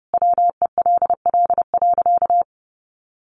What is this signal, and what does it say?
Morse code, heard as a keyed tone, about 30 wpm, 700 Hz. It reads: WELL.
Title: Sentences from Top 1300 Words 30wpm